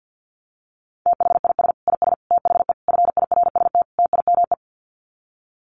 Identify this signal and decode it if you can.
Morse code, heard as a keyed tone, about 50 wpm, 700 Hz. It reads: THIS IS THE FIRST TIME
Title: Sets of 5 Words 50wpm